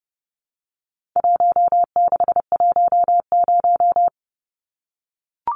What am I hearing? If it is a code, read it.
Morse code, heard as a keyed tone, about 30 wpm, 700 Hz. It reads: 1610
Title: Numbers - 4-Digit 30wpm